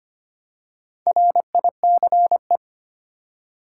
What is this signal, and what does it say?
Morse code, heard as a keyed tone, about 25 wpm, 700 Hz. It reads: RICE
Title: Four Letter Words - 25wpm